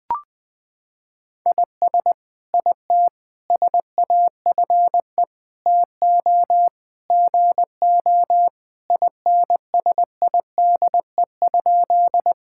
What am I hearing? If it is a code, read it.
Morse code, heard as a keyed tone, about 20 wpm, 700 Hz. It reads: IS IT SAFE TO GO INSIDE?
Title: Sentences from Top 1000 Words - Review Part 6 - 20wpm